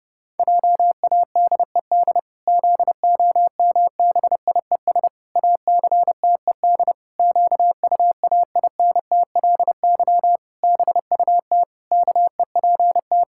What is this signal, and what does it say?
Morse code, heard as a keyed tone, about 30 wpm, 700 Hz. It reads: JADED ZOMBIES ACTED QUAINTLY BUT KEPT
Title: Pangram Challenge v2 - 30wpm